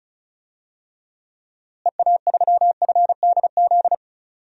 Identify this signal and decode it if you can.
Morse code, heard as a keyed tone, about 36 wpm, 700 Hz. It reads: EA3FDZ
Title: DXCC-100 Call Signs - 6 Characters - Repeated 3x - 35wpm